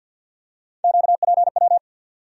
Morse code, heard as a keyed tone, about 52 wpm, 700 Hz. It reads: QPW
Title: USA Call Sign Postfixes - 3 Letters 50wpm